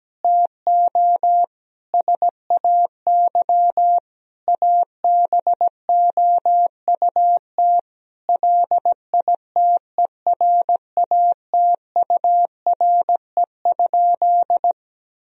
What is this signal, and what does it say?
Morse code, heard as a keyed tone, about 17 wpm, 700 Hz. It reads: TO SAY ABOUT LITERATURE?
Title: Sentences from Top 1000 Words - Review Part 3 - 17wpm